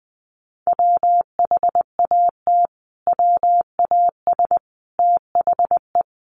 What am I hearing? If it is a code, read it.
Morse code, heard as a keyed tone, about 20 wpm, 700 Hz. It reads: WHAT WAS THE
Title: Sentences from Top 300 Words 20wpm